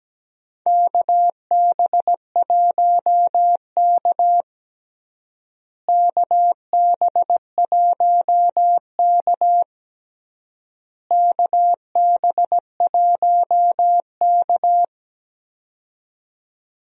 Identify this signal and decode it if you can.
Morse code, heard as a keyed tone, about 17 wpm, 700 Hz. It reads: KB1K KB1K KB1K
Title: USA Call Signs - 4 Characters - Repeated 3x - 17wpm